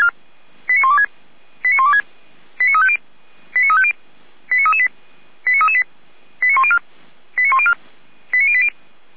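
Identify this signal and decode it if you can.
Five-tone selective call (ZVEI), heard as quick runs of five tones, tones of about 70 ms. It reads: 89126 89126 89350 89350 89338 89338 89104 89104 89080
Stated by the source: ZVEI five-tone selective call received on an SDR, from feuerwehr-orlamuende.org fire brigade audio archive.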